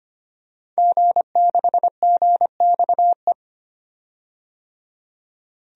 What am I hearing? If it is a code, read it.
Morse code, heard as a keyed tone, about 25 wpm, 700 Hz. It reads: G6GXE
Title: DXCC-100 Call Signs - 5 Characters - 25wpm